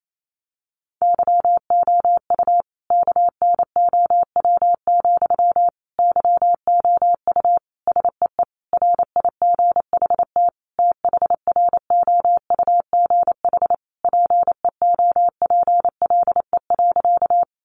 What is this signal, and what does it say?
Morse code, heard as a keyed tone, about 28 wpm, 700 Hz. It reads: YOU KNOW, YOU SEE RIGHT THROUGH PEOPLE.
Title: Sentences from Top 200 Words 28wpm